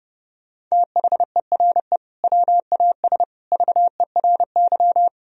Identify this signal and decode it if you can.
Morse code, heard as a keyed tone, about 30 wpm, 700 Hz. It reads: THERE WAS VERY
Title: Sentences from Top 1000 Words - Review Part 1 - 30wpm